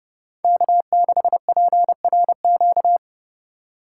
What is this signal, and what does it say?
Morse code, heard as a keyed tone, about 31 wpm, 700 Hz. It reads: K6PRQ
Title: USA Call Signs - 5 Characters - 4x Character Spacing - 30wpm